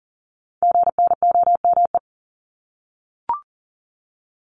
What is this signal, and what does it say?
Morse code, heard as a keyed tone, about 41 wpm, 700 Hz. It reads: GNOME
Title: Multi-letter Phonemes 40wpm